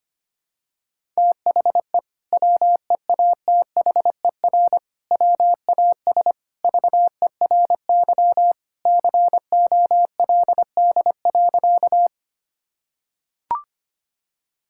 Morse code, heard as a keyed tone, about 25 wpm, 700 Hz. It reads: THE WEATHER WAS VERY COLD.
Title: Sentences from Top 1000 Words - Review Part 5 - 25wpm